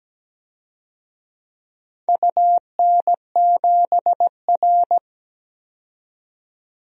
Morse code, heard as a keyed tone, about 17 wpm, 700 Hz. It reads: UN7R
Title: DXCC-100 Call Signs - 4 Characters - 17wpm